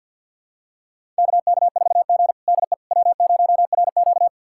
Morse code, heard as a keyed tone, about 46 wpm, 700 Hz. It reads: KK4G DE W0RQ
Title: USA Callsign DE Callsign - 4 Characters - 50wpm